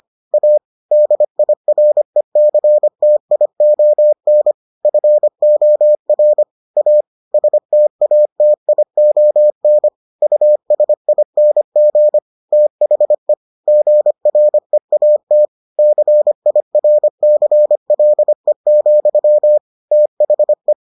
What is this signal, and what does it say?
Morse code, heard as a keyed tone, about 25 wpm, 600 Hz. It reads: A DIRECTION FOR A STATION USING THE GREAT CIRCLE, THE